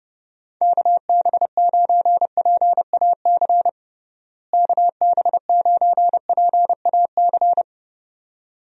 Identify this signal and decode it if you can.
Morse code, heard as a keyed tone, about 30 wpm, 700 Hz. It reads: KB9PAC KB9PAC
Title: USA Call Signs - 6 Characters - Repeated 2x - 30wpm